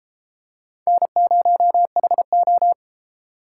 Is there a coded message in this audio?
Yes: Morse code from 0.9 to 2.7 s, heard as a keyed tone, about 33 wpm, 700 Hz. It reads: N0HO